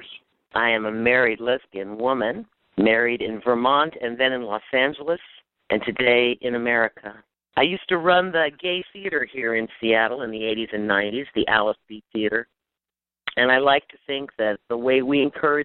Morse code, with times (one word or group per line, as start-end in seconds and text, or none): none